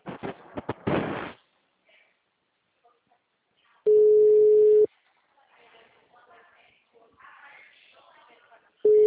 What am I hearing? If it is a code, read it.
Ringback tone (Europe, 425 Hz), with speech.
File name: out- -2002- - - .53658